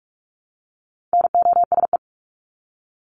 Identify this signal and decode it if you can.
Morse code, heard as a keyed tone, about 44 wpm, 700 Hz. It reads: NOSE
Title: Top 1501-1750 Words 45wpm